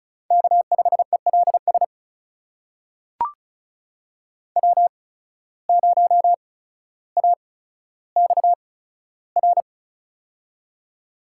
Morse code, heard as a keyed tone, about 36 wpm, spaced as for 15 wpm, 700 Hz. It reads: K5ELS W0AXR
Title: USA Call Signs - 5 Characters - 8x Character Spacing - 35wpm